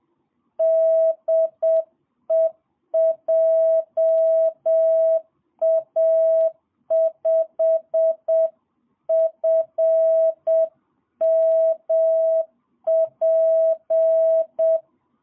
Morse code, heard as a keyed tone, about 7 wpm, 650 Hz. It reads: DE JA5FMP
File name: jh8cbh / 2022cw